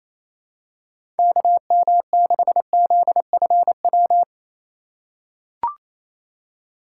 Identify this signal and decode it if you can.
Morse code, heard as a keyed tone, about 28 wpm, 700 Hz. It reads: KM6ZFW
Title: USA Call Signs - 6 Characters - 2x Character Spacing - 28wpm